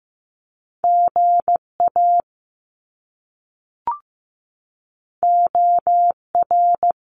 Morse code, heard as a keyed tone, about 15 wpm, 700 Hz. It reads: GA OR
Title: Top Two Letter Combinations in Words 15wpm